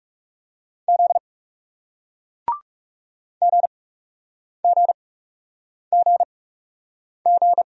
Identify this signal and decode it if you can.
Morse code, heard as a keyed tone, about 40 wpm, 700 Hz. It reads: Z G G G G